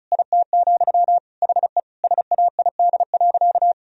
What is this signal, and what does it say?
Morse code, heard as a keyed tone, about 36 wpm, 700 Hz. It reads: IT, HE SAID.
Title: Sentences from Top 1000 Words - Review Part 4 - 35wpm